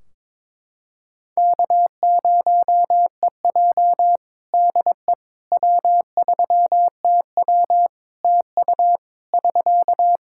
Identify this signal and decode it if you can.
Morse code, heard as a keyed tone, about 22 wpm, 700 Hz. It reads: K0EJ DE W3TW TU <SK>
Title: Practice QSO - Exchange 4 - Ending - 22wpm